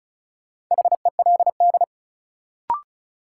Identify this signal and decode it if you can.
Morse code, heard as a keyed tone, about 36 wpm, 700 Hz. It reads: HELD
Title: Sentences from Top 1600 Words - New Word Drill Alt - 35wpm